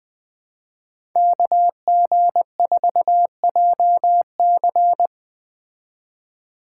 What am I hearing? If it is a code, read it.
Morse code, heard as a keyed tone, about 21 wpm, 700 Hz. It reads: KG4JC